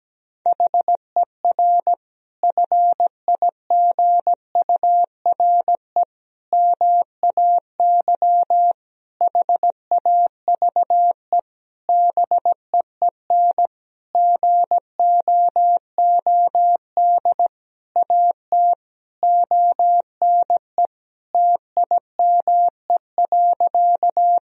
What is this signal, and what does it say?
Morse code, heard as a keyed tone, about 17 wpm, 700 Hz. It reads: HER FIGURE MAY HAVE BEEN GOOD AT ONE TIME.
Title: Sentences from Top 1000 Words - Review Part 2 - 17wpm